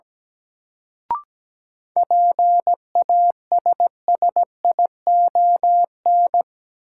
Morse code, heard as a keyed tone, about 17 wpm, 700 Hz. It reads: PASSION
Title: Sentences from Top 1300 Words - New Word Drill Alt - 17wpm